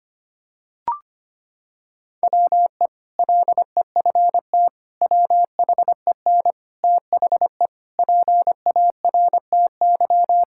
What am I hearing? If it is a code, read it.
Morse code, heard as a keyed tone, about 25 wpm, 700 Hz. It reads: WE LEFT WHEN THE PARTY
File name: practice / 25wpm / Sentences with 2-Word Phrases - 25wpm